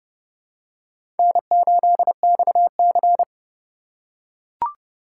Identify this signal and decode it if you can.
Morse code, heard as a keyed tone, about 32 wpm, 700 Hz. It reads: N8XC
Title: USA Call Signs - 4 Characters - 4x Character Spacing - 30wpm